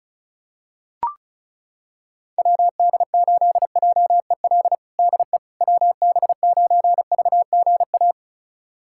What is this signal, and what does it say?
Morse code, heard as a keyed tone, about 36 wpm, 700 Hz. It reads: WD8JEL DE WB9VGA
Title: USA Callsign DE Callsign - 6 Characters - 35wpm